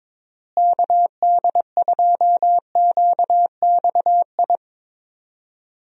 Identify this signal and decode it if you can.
Morse code, heard as a keyed tone, about 22 wpm, 700 Hz. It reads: KD2QXI
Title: USA Call Signs - 6 Characters - Repeated 3x - 22wpm